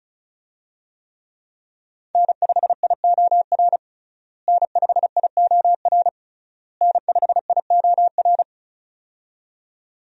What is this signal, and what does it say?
Morse code, heard as a keyed tone, about 34 wpm, 700 Hz. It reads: N5IOR N5IOR N5IOR
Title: USA Call Signs - 5 Characters - Repeated 3x - 35wpm